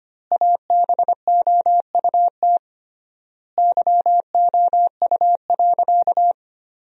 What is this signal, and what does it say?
Morse code, heard as a keyed tone, about 25 wpm, 700 Hz. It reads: ABOUT YOU.
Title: 3x Word-Spacing - Sentences from Top 400 Words - 25wpm